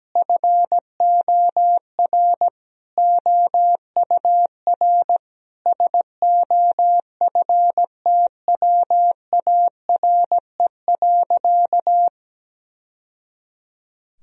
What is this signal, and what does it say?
Morse code, heard as a keyed tone, about 17 wpm, 700 Hz. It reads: FOR OUR SOFTWARE.
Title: Sentences from Top 1000 Words - Review Part 3 - 17wpm